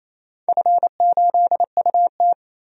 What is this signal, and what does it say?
Morse code, heard as a keyed tone, about 28 wpm, 700 Hz. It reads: F8UT
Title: DXCC-100 Call Signs - 4 Characters - 28wpm